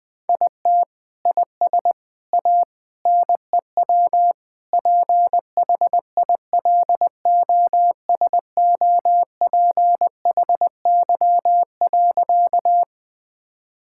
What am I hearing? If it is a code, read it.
Morse code, heard as a keyed tone, about 20 wpm, 700 Hz. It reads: IT IS A NEW PHILOSOPHY.
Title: Sentences from Top 1000 Words - Review Part 6 - 20wpm